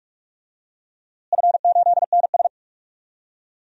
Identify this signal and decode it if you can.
Morse code, heard as a keyed tone, about 46 wpm, 700 Hz. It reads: F8NS
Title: DXCC-100 Call Signs - 4 Characters - Repeated 2x - 45wpm